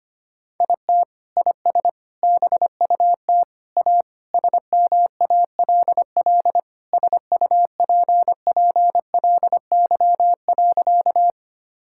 Morse code, heard as a keyed tone, about 25 wpm, 700 Hz. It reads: IT IS BUT A SMALL SUPPLY.
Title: Sentences from Top 1000 Words - Review Part 5 - 25wpm